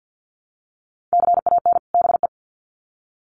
Morse code, heard as a keyed tone, about 50 wpm, 700 Hz. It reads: CAN BE
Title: Sets of 2 Words 50wpm